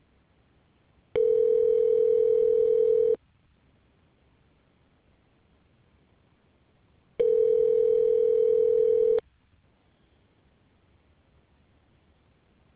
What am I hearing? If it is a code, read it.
Ringback tone (North America), 440 + 480 Hz.